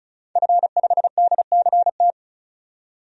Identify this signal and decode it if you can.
Morse code, heard as a keyed tone, about 34 wpm, 700 Hz. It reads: F5DCT
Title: DXCC-100 Call Signs - 5 Characters - 35wpm